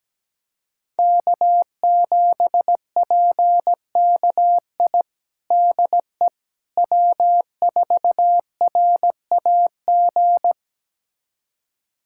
Morse code, heard as a keyed tone, about 17 wpm, 700 Hz. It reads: K7PKI DE W4RAG
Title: USA Callsign DE Callsign - 5 Characters - 17wpm